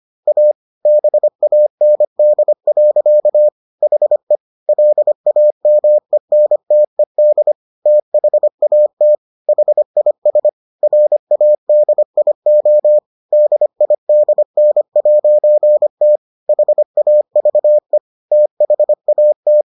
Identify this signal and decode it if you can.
Morse code, heard as a keyed tone, about 25 wpm, 600 Hz. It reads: A BAND. HE LAMENTED THAT HIS RADIO DIDN'T HAVE THAT